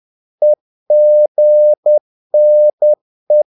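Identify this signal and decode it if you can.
Morse code, heard as a keyed tone, about 10 wpm, 600 Hz. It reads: EGNE
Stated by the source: Morse practice of short Danish words at 10 wpm, from CW med Gnister.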